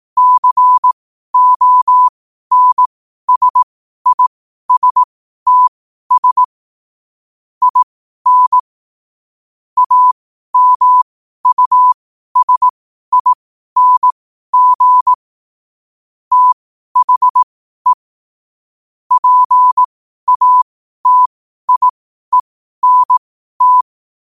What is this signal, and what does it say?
Morse code, heard as a keyed tone, about 18 wpm, spaced as for 12 wpm, 1000 Hz. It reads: CONSISTS IN AMUSING THE PATIENT
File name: 12wpm